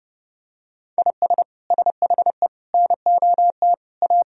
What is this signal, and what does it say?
Morse code, heard as a keyed tone, about 32 wpm, 700 Hz. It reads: IS SHE NOT A
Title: Sentences from Top 1000 Words - Review Part 2 - 30wpm